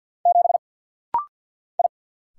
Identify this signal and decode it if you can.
Morse code, heard as a keyed tone, about 50 wpm, 700 Hz. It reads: 7 I
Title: Single Letter-Number - Rapid-Fire 50wpm